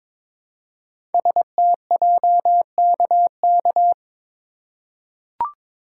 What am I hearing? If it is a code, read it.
Morse code, heard as a keyed tone, about 22 wpm, 700 Hz. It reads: STJKK
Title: Code Groups - 5 Letters - 22wpm